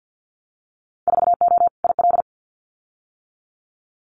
Morse code, heard as a keyed tone, about 50 wpm, 700 Hz. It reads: 4O IL
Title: Winter Field Day Exchanges Encore 50wpm